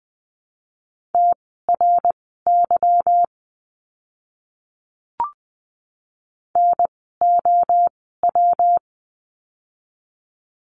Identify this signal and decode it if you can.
Morse code, heard as a keyed tone, about 20 wpm, spaced as for 14 wpm, 700 Hz. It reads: TRY NOW
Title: 160 Lesson 26f - Letter g - Words 20wpm